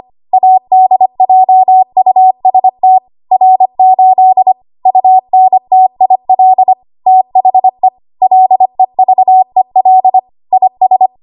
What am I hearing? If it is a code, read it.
Morse code, heard as a keyed tone, about 25 wpm, 750 Hz. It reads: ADJUST R8 UNTIL THE LEVEL IS
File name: Archive / 25 WPM / 25WPM